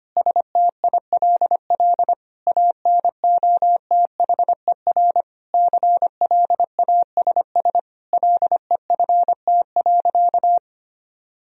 Morse code, heard as a keyed tone, about 25 wpm, 700 Hz. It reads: STILL ANOTHER CLASS LEFT.
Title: Sentences from Top 1000 Words - Review Part 2 - 25wpm